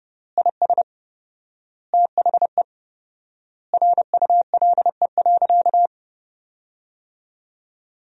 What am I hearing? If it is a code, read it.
Morse code, heard as a keyed tone, about 30 wpm, 700 Hz. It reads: IS THE RULE.
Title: 4x Word-Spacing - Sentences from Top 400 Words - 30wpm